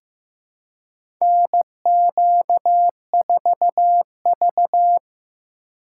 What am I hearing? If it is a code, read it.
Morse code, heard as a keyed tone, about 15 wpm, 700 Hz. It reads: NQ4V
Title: USA Call Signs - 2x Character Spacing - 15wpm